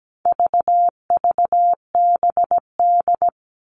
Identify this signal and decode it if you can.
Morse code, heard as a keyed tone, about 17 wpm, 700 Hz. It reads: VVBD
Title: All Cut Numbers - 4-Digit 17wpm